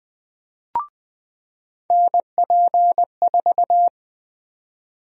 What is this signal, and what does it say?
Morse code, heard as a keyed tone, about 21 wpm, 700 Hz. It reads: NP4
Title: USA Call Sign Prefixes - 20wpm